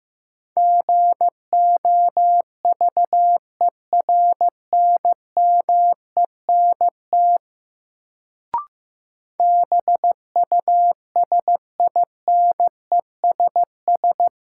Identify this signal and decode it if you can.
Morse code, heard as a keyed tone, about 15 wpm, 700 Hz. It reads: GOVERNMENT BUSINESS
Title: Sentences from Top 600 Words - New Word Drill - 15wpm